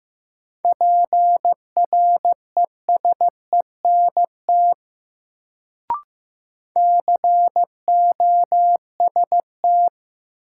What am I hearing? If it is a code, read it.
Morse code, heard as a keyed tone, about 15 wpm, 700 Hz. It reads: PRESENT COST
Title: Sentences from Top 800 Words - New Word Drill - 15wpm